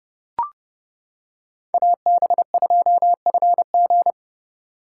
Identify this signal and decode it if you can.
Morse code, heard as a keyed tone, about 28 wpm, 700 Hz. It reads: AB2FG